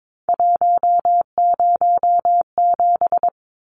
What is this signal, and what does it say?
Morse code, heard as a keyed tone, about 22 wpm, 700 Hz. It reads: 107
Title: Numbers - 3-Digit 22wpm